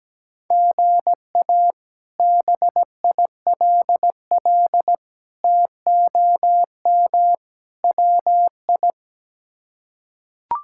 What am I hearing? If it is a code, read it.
Morse code, heard as a keyed tone, about 17 wpm, 700 Hz. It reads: GA BILL TOM WI
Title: K1USN SST QSO Practice - 17wpm